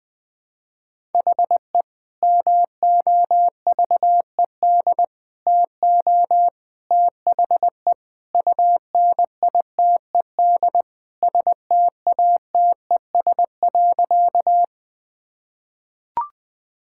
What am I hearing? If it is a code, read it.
Morse code, heard as a keyed tone, about 20 wpm, 700 Hz. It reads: HE MOVED TO THE UNITED STATES.
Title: Sentences with 3-Word Phrases - 20wpm